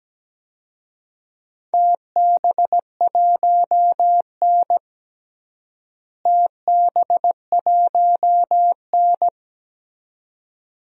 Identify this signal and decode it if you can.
Morse code, heard as a keyed tone, about 17 wpm, 700 Hz. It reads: TB1N TB1N
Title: DXCC-100 Call Signs - 4 Characters - Repeated 2x - 17wpm